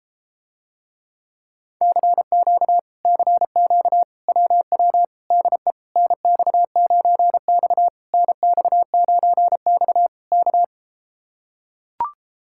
Morse code, heard as a keyed tone, about 33 wpm, 700 Hz. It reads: CQ CQ WW DE NX9X NX9X K